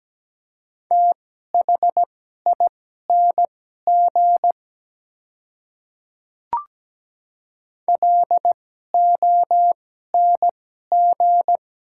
Morse code, heard as a keyed tone, about 17 wpm, spaced as for 12 wpm, 700 Hz. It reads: THING LONG
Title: 2x Character Spacing - Top 300 Words - 17wpm